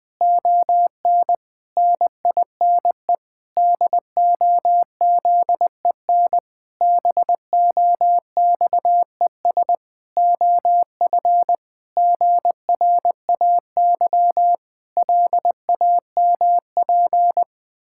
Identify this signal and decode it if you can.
Morse code, heard as a keyed tone, about 20 wpm, 700 Hz. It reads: ON NINE DOZEN BOXES OF GRAY LAMP
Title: Pangram Challenge v2 - 20wpm